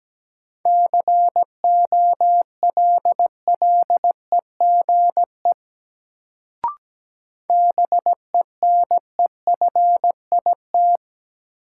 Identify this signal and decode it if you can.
Morse code, heard as a keyed tone, about 17 wpm, 700 Hz. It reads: COLLEGE BENEFIT
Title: Sentences from Top 1300 Words - New Word Drill Alt - 17wpm